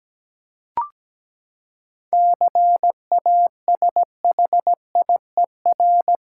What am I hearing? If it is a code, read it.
Morse code, heard as a keyed tone, about 17 wpm, 700 Hz. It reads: CASHIER
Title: Practice QSO - Exchange 3 - More Info - 17wpm